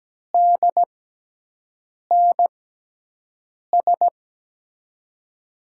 Morse code, heard as a keyed tone, about 17 wpm, spaced as for 6 wpm, 700 Hz. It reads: DNS
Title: USA Call Signs - 5 Characters - 6x Character Spacing - 17wpm